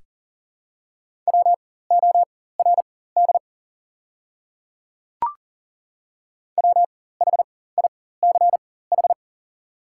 Morse code, heard as a keyed tone, about 40 wpm, spaced as for 19 wpm, 690 Hz. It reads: WORD WHICH
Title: 4x Character Spacing - Top 100 Words - 40wpm